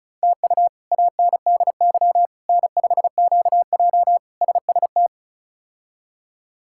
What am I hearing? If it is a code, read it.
Morse code, heard as a keyed tone, about 36 wpm, 700 Hz. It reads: TU ANDY N5QJ SST
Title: K1USN SST QSO Practice - 35wpm